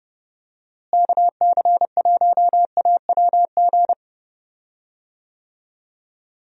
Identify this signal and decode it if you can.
Morse code, heard as a keyed tone, about 29 wpm, 700 Hz. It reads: KC1AWG
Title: USA Call Signs - 6 Characters - Repeated 3x - 30wpm